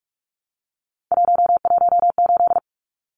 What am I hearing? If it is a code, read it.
Morse code, heard as a keyed tone, about 44 wpm, 700 Hz. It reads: 118
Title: Numbers - 3-Digit 45wpm